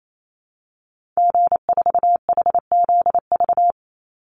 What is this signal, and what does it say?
Morse code, heard as a keyed tone, about 28 wpm, 700 Hz. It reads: G4HZV
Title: Call Signs Repeated 3x - 28wpm